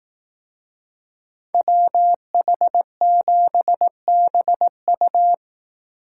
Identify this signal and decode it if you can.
Morse code, heard as a keyed tone, about 18 wpm, 700 Hz. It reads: WH7BU
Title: Speed Racing - USA Call Signs 15wpm